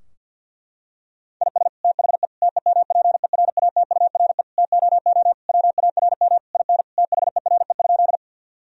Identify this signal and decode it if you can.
Morse code, heard as a keyed tone, about 50 wpm, 700 Hz. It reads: IS THE TEMPERATURE TOO WARM IN THERE?
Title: Sentences from Top 1000 Words - Review Part 4 - 50wpm